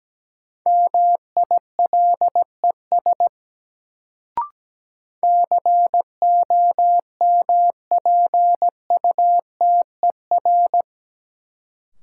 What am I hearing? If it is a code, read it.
Morse code, heard as a keyed tone, about 17 wpm, 700 Hz. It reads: MILES COMPUTER
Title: Sentences from Top 1000 Words - Review Part 3 - New Word Drill - 17wpm